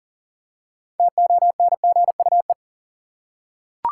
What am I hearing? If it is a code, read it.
Morse code, heard as a keyed tone, about 38 wpm, 700 Hz. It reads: TONGUE
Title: Six Letter Words - 40wpm